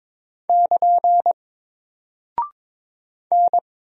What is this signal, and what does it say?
Morse code, heard as a keyed tone, about 22 wpm, 700 Hz. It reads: <KN> N
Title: Single Character (Letter:Number:Punctuation:Prosign) Alt 22wpm